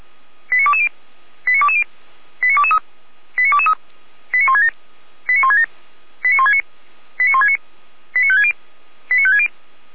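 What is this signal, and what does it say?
Five-tone selective call (ZVEI), heard as quick runs of five tones, tones of about 70 ms. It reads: 89220 89220 89223 89223 89167 89167 89169 89169 89560 89560
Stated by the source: ZVEI five-tone selective call for 89017, received by SDR.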